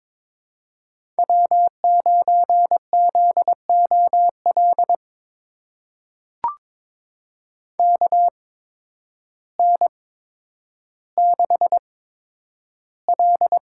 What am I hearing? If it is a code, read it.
Morse code, heard as a keyed tone, about 22 wpm, spaced as for 10 wpm, 700 Hz. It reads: W9ZOL KN6L
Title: USA Call Signs - 5 Characters - 8x Character Spacing - 22wpm